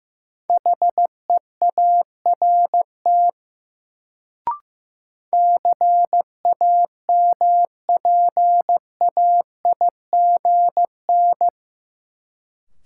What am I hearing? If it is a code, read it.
Morse code, heard as a keyed tone, about 15 wpm, 700 Hz. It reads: HEART CAMPAIGN
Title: Sentences from Top 1200 Words - New Word Drill Alt - 15wpm